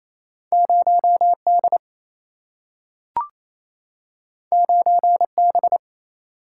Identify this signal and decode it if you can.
Morse code, heard as a keyed tone, about 28 wpm, 700 Hz. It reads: 0 D 9 B
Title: Single-Letter Single-Number 28wpm